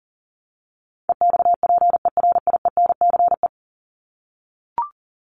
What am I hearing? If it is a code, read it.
Morse code, heard as a keyed tone, about 38 wpm, 700 Hz. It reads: EXPERIENCE
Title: Top 501-750 Words 40wpm